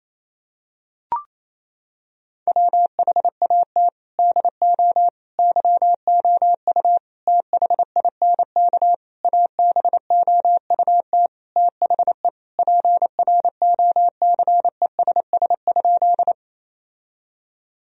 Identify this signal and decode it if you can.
Morse code, heard as a keyed tone, about 28 wpm, 700 Hz. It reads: WHAT DO YOU THINK ABOUT THE PROCESS?
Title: Sentences from Top 1000 Words - Review Part 3 - 28wpm